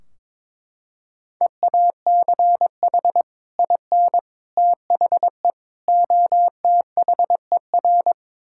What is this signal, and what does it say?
Morse code, heard as a keyed tone, about 22 wpm, 700 Hz. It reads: EACH IN THE OTHER
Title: Sentences from Top 1600 Words 22wpm